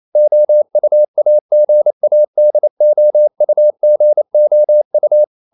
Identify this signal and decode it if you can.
Morse code, heard as a keyed tone, about 28 wpm, 600 Hz. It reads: OUAGADOUGOU
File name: CW / Practice / NationalCapitals28